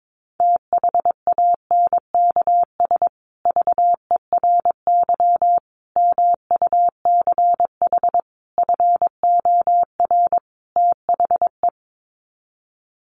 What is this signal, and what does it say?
Morse code, heard as a keyed tone, about 22 wpm, 700 Hz. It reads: THANKS VERY MUCH FOR THE
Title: Sets of 5 Words Encore2 22wpm